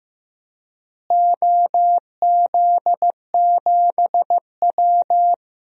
Morse code, heard as a keyed tone, about 15 wpm, 700 Hz. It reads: OZ7W
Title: DXCC-100 Call Signs - 4 Characters - Repeated 3x - 15wpm